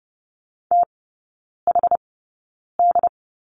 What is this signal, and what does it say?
Morse code, heard as a keyed tone, about 30 wpm, spaced as for 9 wpm, 700 Hz. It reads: THD
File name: 081 Lesson 14e - Number 2 - Three Characters 30wpm